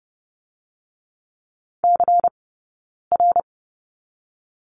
Morse code, heard as a keyed tone, about 30 wpm, 700 Hz. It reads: C R
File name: 092 Lesson 16d - Letter c - Two Characters 30wpm